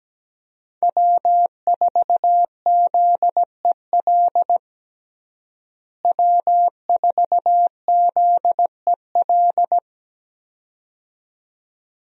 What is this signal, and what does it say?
Morse code, heard as a keyed tone, about 17 wpm, 700 Hz. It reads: W4ZEL W4ZEL
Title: USA Call Signs - 5 Characters - Repeated 2x - 17wpm